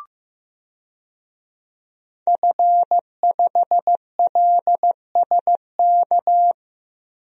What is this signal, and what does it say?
Morse code, heard as a keyed tone, about 16 wpm, 700 Hz. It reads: F5LSK
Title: DXCC-100 Call Signs - 5 Characters - 15wpm